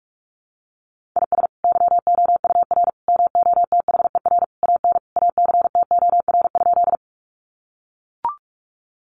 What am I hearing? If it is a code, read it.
Morse code, heard as a keyed tone, about 44 wpm, 700 Hz. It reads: IS YOUR MOTHER AN ACTOR?
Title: Sentences from Top 1500 Words 45wpm